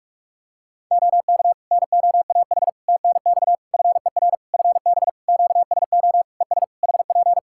Morse code, heard as a keyed tone, about 46 wpm, 700 Hz. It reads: OK NOAH TNX FER FB QSO ES HP
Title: Practice QSO - Exchange 4 - Ending - 45wpm